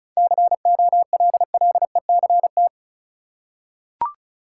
Morse code, heard as a keyed tone, about 36 wpm, 700 Hz. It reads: COLLECT
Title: Seven Letter Words - 35wpm